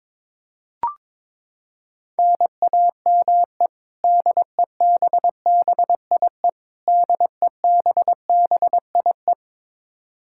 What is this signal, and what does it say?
Morse code, heard as a keyed tone, about 22 wpm, 700 Hz. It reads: NAME DEBBIE DEBBIE
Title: Practice QSO - Exchange 1 - Beginning - 22wpm